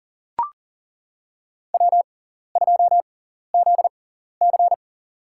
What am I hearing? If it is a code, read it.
Morse code, heard as a keyed tone, about 40 wpm, spaced as for 14 wpm, 700 Hz. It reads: W2ZC
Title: USA Call Signs - 4 Characters - 6x Character Spacing - 40wpm